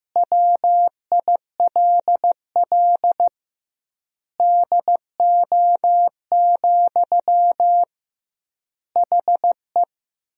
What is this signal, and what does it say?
Morse code, heard as a keyed tone, about 15 wpm, 700 Hz. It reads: WILL DO, HE
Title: 2x Word-Spacing - Sentences from Top 100 Words - 15wpm